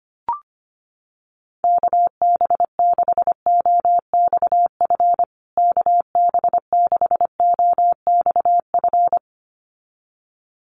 Morse code, heard as a keyed tone, about 25 wpm, 700 Hz. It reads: KB6OXF KB6OXF
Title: USA Call Signs Repeated 2x 25wpm